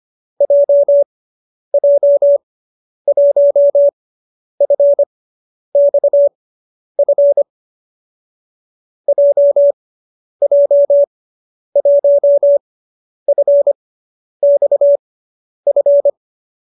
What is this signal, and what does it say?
Morse code, heard as a keyed tone, about 25 wpm, spaced as for 10 wpm, 570 Hz. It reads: JJ1FXF JJ1FXF